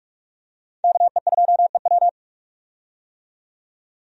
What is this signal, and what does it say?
Morse code, heard as a keyed tone, about 44 wpm, 700 Hz. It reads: KE2EW
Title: USA Call Signs - 5 Characters - 45wpm